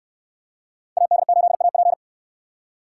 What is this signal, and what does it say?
Morse code, heard as a keyed tone, about 69 wpm, 700 Hz. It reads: AD8AP